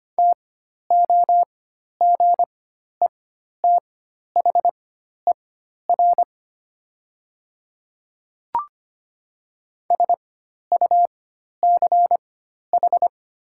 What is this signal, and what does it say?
Morse code, heard as a keyed tone, about 25 wpm, spaced as for 12 wpm, 700 Hz. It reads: TOGETHER SUCH
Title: 4x Character Spacing - Top 300 Words - 25wpm